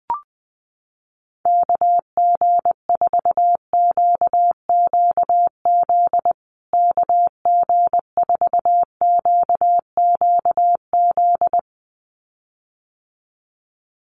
Morse code, heard as a keyed tone, about 20 wpm, 700 Hz. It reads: KG4QQZ KG4QQZ
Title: USA Call Signs Repeated 2x 20wpm